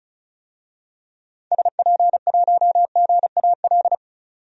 Morse code, heard as a keyed tone, about 34 wpm, 700 Hz. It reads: SP1GAL
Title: DXCC-100 Call Signs - 6 Characters - 35wpm